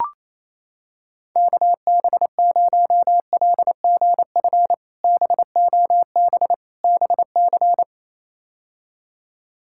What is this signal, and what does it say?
Morse code, heard as a keyed tone, about 28 wpm, 700 Hz. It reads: KB0LGF BOB BC
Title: K1USN SST QSO Practice - 28wpm